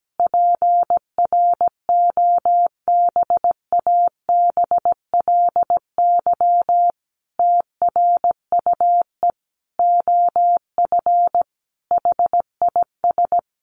Morse code, heard as a keyed tone, about 17 wpm, 700 Hz. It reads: PROBABLY TRUE OF HIS
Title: Sentences from Top 700 Words 17wpm